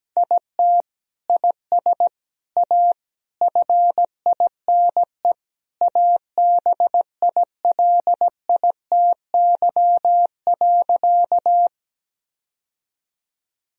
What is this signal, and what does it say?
Morse code, heard as a keyed tone, about 17 wpm, 700 Hz. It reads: IT IS A FINE ABILITY.
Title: Sentences from Top 1000 Words - Review Part 3 - 17wpm